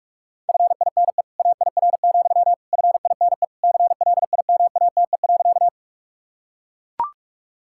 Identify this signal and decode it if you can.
Morse code, heard as a keyed tone, about 41 wpm, 700 Hz. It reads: FINE AIR, FINE CLIMATE.
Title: Sentences from Top 1600 Words 45wpm